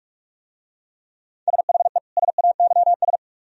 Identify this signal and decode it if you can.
Morse code, heard as a keyed tone, about 44 wpm, 700 Hz. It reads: SHE SAYS